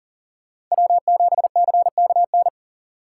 Morse code, heard as a keyed tone, about 39 wpm, 700 Hz. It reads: W7CKN
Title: USA Call Signs - 5 Characters - 3x Character Spacing - 40wpm